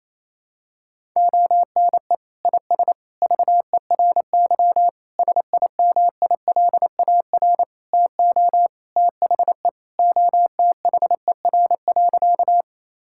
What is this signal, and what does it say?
Morse code, heard as a keyed tone, about 27 wpm, 700 Hz. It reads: ONE IS VERY SIMILAR TO THE OTHER.
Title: Sentences from Top 1000 Words - Review Part 5 - 28wpm